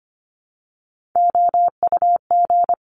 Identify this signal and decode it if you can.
Morse code, heard as a keyed tone, about 25 wpm, 700 Hz. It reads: OUG
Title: Top Three Letter Combinations in Words 25wpm